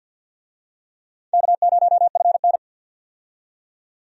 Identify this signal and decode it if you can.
Morse code, heard as a keyed tone, about 52 wpm, 700 Hz. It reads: K0FN
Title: USA Call Signs - 4 Characters - 50wpm